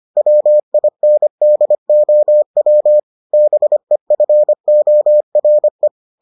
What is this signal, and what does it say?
Morse code, heard as a keyed tone, about 25 wpm, 600 Hz. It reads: WINDOW BEFORE